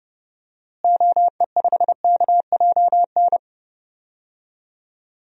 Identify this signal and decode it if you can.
Morse code, heard as a keyed tone, about 32 wpm, 700 Hz. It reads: OE5KJN
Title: DXCC-100 Call Signs - 6 Characters - Repeated 3x - 30wpm